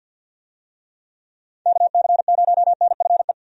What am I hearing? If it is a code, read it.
Morse code, heard as a keyed tone, about 51 wpm, 700 Hz. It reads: KC0NFE
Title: USA Call Signs - 6 Characters - 50wpm